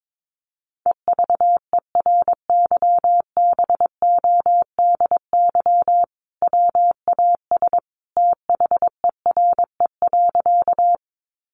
Morse code, heard as a keyed tone, about 22 wpm, 700 Hz. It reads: EVERYBODY WAS THERE.
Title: Sentences from Top 1400 Words 22wpm